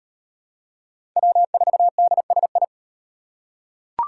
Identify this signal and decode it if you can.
Morse code, heard as a keyed tone, about 38 wpm, 700 Hz. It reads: W4DSI